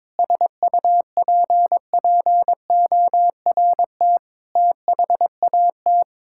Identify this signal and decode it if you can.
Morse code, heard as a keyed tone, about 22 wpm, 700 Hz. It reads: SUPPORT THAT
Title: Sentences from Top 1000 Words - Review Part 5 - 22wpm